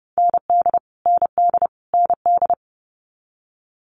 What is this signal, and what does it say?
Morse code, heard as a keyed tone, about 30 wpm, 700 Hz. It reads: ND ND ND
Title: Multi-letter Phonemes Repeated 3x 30wpm